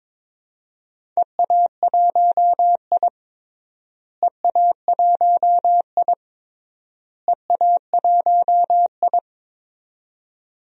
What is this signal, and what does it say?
Morse code, heard as a keyed tone, about 22 wpm, 700 Hz. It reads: EA1I EA1I EA1I
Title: DXCC-100 Call Signs - 4 Characters - Repeated 3x - 22wpm